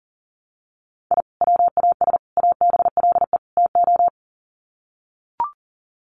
Morse code, heard as a keyed tone, about 40 wpm, 700 Hz. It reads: I WAS ABLE TO
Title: Sets of 4 Words 40wpm